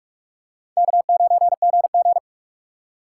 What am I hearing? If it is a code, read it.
Morse code, heard as a keyed tone, about 44 wpm, 700 Hz. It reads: K9GG